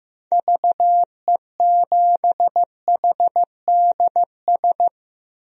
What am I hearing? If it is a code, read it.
Morse code, heard as a keyed tone, about 15 wpm, 700 Hz. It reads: VE7HDS
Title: DXCC-100 Call Signs - 6 Characters - Repeated 2x - 15wpm